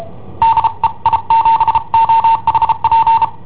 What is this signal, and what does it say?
Morse code, heard as a keyed tone, about 33 wpm, 910 Hz. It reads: DE I7OHP
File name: i7ohp / cassetta i7ohp B